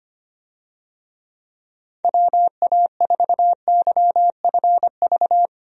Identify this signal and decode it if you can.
Morse code, heard as a keyed tone, about 25 wpm, 700 Hz. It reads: WA4YFV
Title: USA Call Signs - 6 Characters - 25wpm